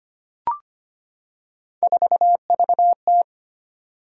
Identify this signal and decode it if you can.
Morse code, heard as a keyed tone, about 25 wpm, 700 Hz. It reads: 4VT
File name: Three Letter-Number Combinations - 25wpm